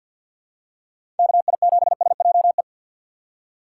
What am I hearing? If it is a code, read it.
Morse code, heard as a keyed tone, about 52 wpm, 700 Hz. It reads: KI7SJE
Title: USA Call Signs - 3x Character Spacing - 50wpm